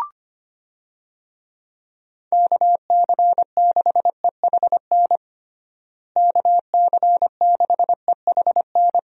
Morse code, heard as a keyed tone, about 25 wpm, 700 Hz. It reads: KC6EHN KC6EHN
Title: USA Call Signs - 6 Characters - Repeated 3x - 25wpm